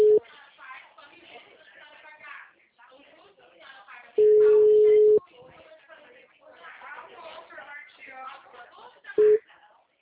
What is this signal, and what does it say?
Ringback tone (Europe, 425 Hz), with speech.
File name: out- -2022- - - .20657